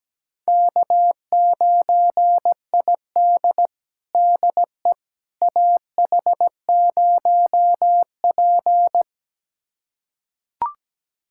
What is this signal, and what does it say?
Morse code, heard as a keyed tone, about 17 wpm, 700 Hz. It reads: K9ID DE AH0P
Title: USA Callsign DE Callsign - 4 Characters - 17wpm